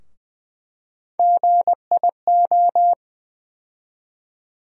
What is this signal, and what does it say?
Morse code, heard as a keyed tone, about 20 wpm, 700 Hz. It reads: GIO
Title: USA Call Sign Postfixes - 3 Letters 20wpm